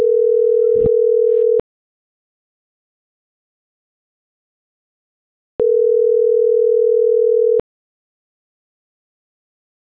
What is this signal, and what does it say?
Ringback tone (North America), 440 + 480 Hz.